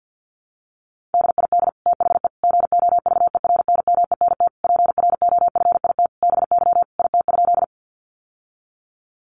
Morse code, heard as a keyed tone, about 50 wpm, 700 Hz. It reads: DID THE GOVERNMENT PROFIT BY IT?
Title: Sentences from Top 800 Words 50wpm